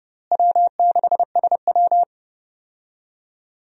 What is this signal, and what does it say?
Morse code, heard as a keyed tone, about 30 wpm, 700 Hz. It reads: W6SW
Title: USA Call Signs - 4 Characters - Repeated 2x - 30wpm